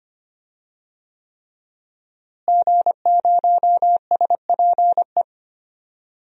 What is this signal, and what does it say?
Morse code, heard as a keyed tone, about 25 wpm, 700 Hz. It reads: G0SPE
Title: DXCC-100 Call Signs - 5 Characters - Repeated 3x - 25wpm